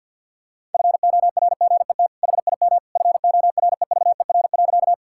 Morse code, heard as a keyed tone, about 50 wpm, 700 Hz. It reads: FORGET HIM FOREVER.
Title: Sentences from Top 1600 Words 50wpm